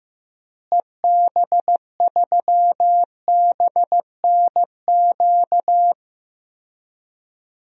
Morse code, heard as a keyed tone, about 15 wpm, 700 Hz. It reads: EB3BNQ
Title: DXCC-100 Call Signs - 6 Characters - Repeated 2x - 15wpm